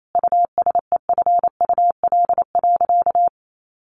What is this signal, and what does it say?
Morse code, heard as a keyed tone, about 28 wpm, 700 Hz. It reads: USEFUL.
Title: Sentences from Top 1400 Words 28wpm